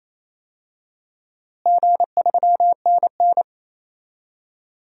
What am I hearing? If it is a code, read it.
Morse code, heard as a keyed tone, about 28 wpm, 700 Hz. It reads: G3NN
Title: DXCC-100 Call Signs - 4 Characters - Repeated 2x - 28wpm